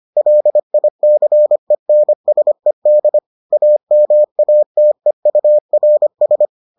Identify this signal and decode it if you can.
Morse code, heard as a keyed tone, about 25 wpm, 600 Hz. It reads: LICENSED AMATEURS